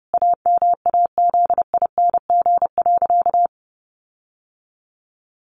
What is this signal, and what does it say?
Morse code, heard as a keyed tone, about 30 wpm, 700 Hz. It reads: AMAZING.
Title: Sentences from Top 1500 Words 30wpm